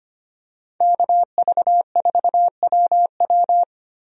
Morse code, heard as a keyed tone, about 25 wpm, 700 Hz. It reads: KV4WW
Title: USA Call Signs - 5 Characters - Repeated 2x - 25wpm